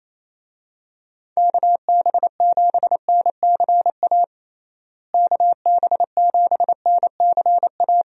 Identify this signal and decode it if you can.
Morse code, heard as a keyed tone, about 27 wpm, 700 Hz. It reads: KB7NCA KB7NCA
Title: USA Call Signs - 6 Characters - Repeated 2x - 28wpm